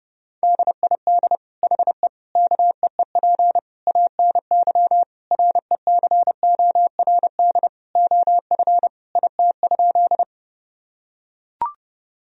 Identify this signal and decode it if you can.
Morse code, heard as a keyed tone, about 30 wpm, 700 Hz. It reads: DID HE KEEP ANY RECORD OF IT?
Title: Sentences from Top 1000 Words - Review Part 3 - 30wpm